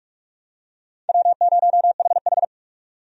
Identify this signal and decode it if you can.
Morse code, heard as a keyed tone, about 46 wpm, 700 Hz. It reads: W0HH
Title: USA Call Signs - 4 Characters - 45wpm